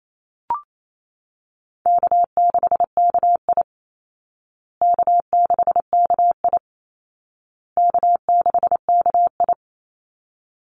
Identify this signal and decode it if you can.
Morse code, heard as a keyed tone, about 28 wpm, 700 Hz. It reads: K6KI K6KI K6KI
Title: Easy Call Signs Repeated 3x - 28wpm